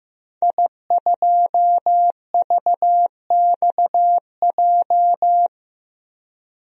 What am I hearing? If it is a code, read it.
Morse code, heard as a keyed tone, about 15 wpm, 700 Hz. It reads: I2VXJ
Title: DXCC-100 Call Signs - 5 Characters - 15wpm